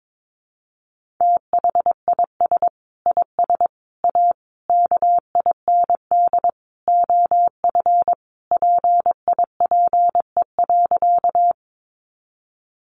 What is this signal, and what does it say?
Morse code, heard as a keyed tone, about 22 wpm, 700 Hz. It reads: THIS IS A KIND OF PIPE.
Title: Sentences from Top 1500 Words 22wpm